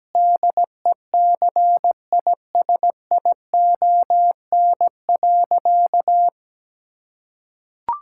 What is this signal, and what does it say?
Morse code, heard as a keyed tone, about 17 wpm, 700 Hz. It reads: DECISION.
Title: Sentences from Top 1600 Words 17wpm